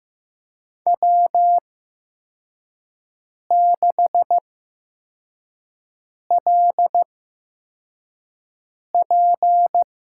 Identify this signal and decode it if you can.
Morse code, heard as a keyed tone, about 15 wpm, spaced as for 4 wpm, 700 Hz. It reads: W6LP
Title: USA Call Signs - 5 Characters - 8x Character Spacing - 15wpm